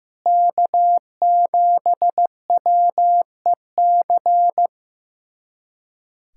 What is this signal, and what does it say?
Morse code, heard as a keyed tone, about 15 wpm, 700 Hz. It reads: K7WEC
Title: USA Callsign DE Callsign - 5 Characters - 15wpm